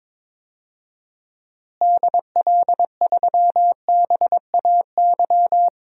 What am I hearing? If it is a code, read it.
Morse code, heard as a keyed tone, about 22 wpm, 700 Hz. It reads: DL3BAY
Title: DXCC-100 Call Signs - 6 Characters - Repeated 3x - 22wpm